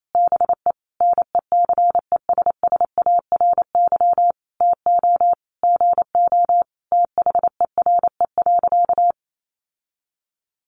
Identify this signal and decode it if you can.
Morse code, heard as a keyed tone, about 28 wpm, 700 Hz. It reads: BE NECESSARY TO GO THERE.
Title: Sentences from Top 800 Words 28wpm